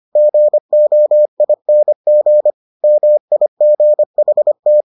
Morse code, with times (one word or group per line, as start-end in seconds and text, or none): 0.2-2.5 GOING
2.8-4.8 MIGHT